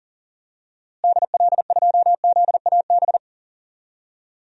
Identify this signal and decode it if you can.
Morse code, heard as a keyed tone, about 40 wpm, 700 Hz. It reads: DL2ZAB
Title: DXCC-100 Call Signs - 6 Characters - Repeated 3x - 40wpm